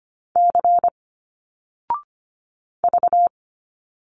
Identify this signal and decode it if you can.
Morse code, heard as a keyed tone, about 25 wpm, 700 Hz. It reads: C V
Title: Single Letters 25wpm